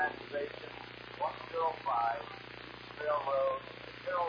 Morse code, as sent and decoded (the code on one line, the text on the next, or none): none